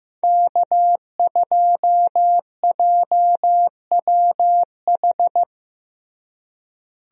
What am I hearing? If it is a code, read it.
Morse code, heard as a keyed tone, about 15 wpm, 700 Hz. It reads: K2JWH
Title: USA Call Signs - 5 Characters - Repeated 2x - 15wpm